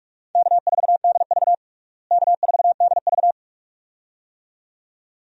Morse code, heard as a keyed tone, about 46 wpm, 700 Hz. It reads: K4DV K4DV
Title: USA Call Signs - 4 Characters - Repeated 2x - 45wpm